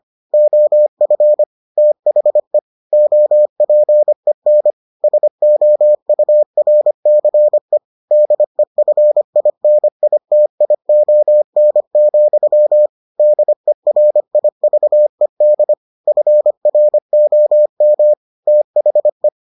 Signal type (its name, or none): Morse code